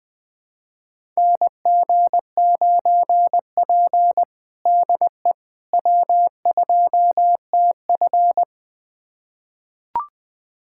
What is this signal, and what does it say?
Morse code, heard as a keyed tone, about 20 wpm, 700 Hz. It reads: NG9P DE W2TF
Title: USA Callsign DE Callsign - 4 Characters - 20wpm